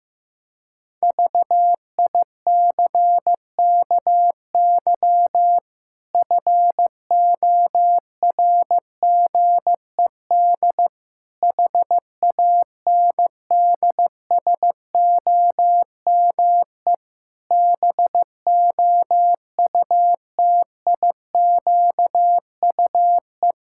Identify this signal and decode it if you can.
Morse code, heard as a keyed tone, about 15 wpm, 700 Hz. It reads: VICKY FORGED HANDSOME BOUTIQUE
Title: Pangram Challenge v2 - 15wpm